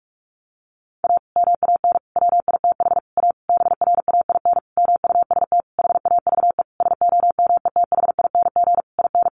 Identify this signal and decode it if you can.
Morse code, heard as a keyed tone, about 44 wpm, 700 Hz. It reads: A MAN WITH A BRAIN MUST HAVE SOMETHING IN
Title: Sentences from Top 1100 Words 45wpm